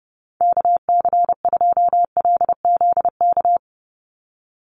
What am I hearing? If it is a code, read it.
Morse code, heard as a keyed tone, about 30 wpm, 700 Hz. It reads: KC2LZK
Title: USA Call Signs 30wpm